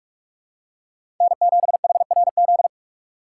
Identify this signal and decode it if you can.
Morse code, heard as a keyed tone, about 43 wpm, 700 Hz. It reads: N7HRZ
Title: USA Call Signs - 5 Characters - 3x Character Spacing - 45wpm